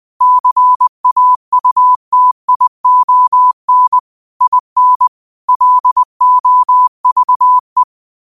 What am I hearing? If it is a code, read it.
Morse code, heard as a keyed tone, about 20 wpm, 1000 Hz. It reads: CAUTION IN LOVE
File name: qotdfiles / 20wpm